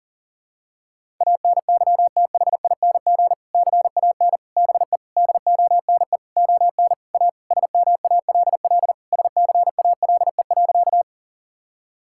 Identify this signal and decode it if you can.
Morse code, heard as a keyed tone, about 40 wpm, 700 Hz. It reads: ANYTHING CAN BE DONE ON A SMALL SCALE.
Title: Sentences from Top 1000 Words - Review Part 5 - 40wpm